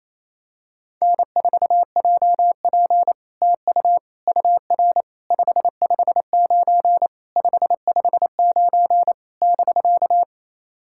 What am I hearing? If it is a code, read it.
Morse code, heard as a keyed tone, about 27 wpm, 700 Hz. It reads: N4JP TU UR 559 559 <BK>
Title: SOTA QSO Practice 28wpm